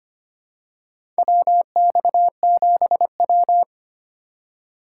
Morse code, heard as a keyed tone, about 25 wpm, 700 Hz. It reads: WX7W
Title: USA Call Signs - 6x Character Spacing - 25wpm